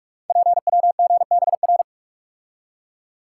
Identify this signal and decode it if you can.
Morse code, heard as a keyed tone, about 44 wpm, 690 Hz. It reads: PWGBR
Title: Code Groups - 5 Letters - 45wpm